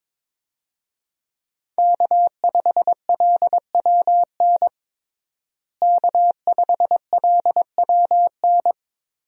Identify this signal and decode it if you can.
Morse code, heard as a keyed tone, about 22 wpm, 700 Hz. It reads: K5LWN K5LWN
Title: USA Call Signs - 5 Characters - Repeated 3x - 22wpm